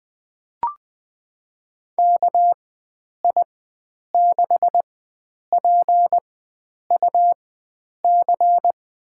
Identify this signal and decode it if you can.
Morse code, heard as a keyed tone, about 20 wpm, spaced as for 9 wpm, 700 Hz. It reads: KI6PUC
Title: USA Call Signs - 6 Characters - 4x Character Spacing - 20wpm